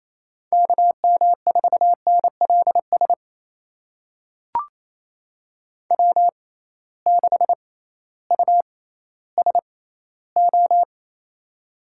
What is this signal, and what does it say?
Morse code, heard as a keyed tone, about 28 wpm, spaced as for 15 wpm, 700 Hz. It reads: KM4NLS W6USO
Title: USA Call Signs - 6x Character Spacing - 28wpm